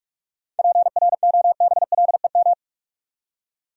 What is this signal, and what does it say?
Morse code, heard as a keyed tone, about 46 wpm, 700 Hz. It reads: PROBLEM
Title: Seven Letter Words - 45wpm